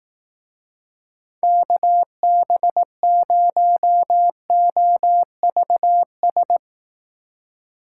Morse code, heard as a keyed tone, about 19 wpm, 700 Hz. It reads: KB0OVS
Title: Speed Racing - USA 2x3 Call Signs 15wpm